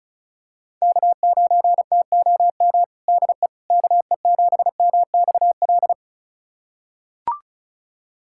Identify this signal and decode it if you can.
Morse code, heard as a keyed tone, about 36 wpm, 700 Hz. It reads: K9TOM DE KE7MXL
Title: USA Callsign DE Callsign 35wpm